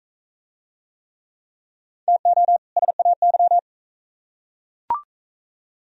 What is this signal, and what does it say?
Morse code, heard as a keyed tone, about 42 wpm, 700 Hz. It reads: TO SAY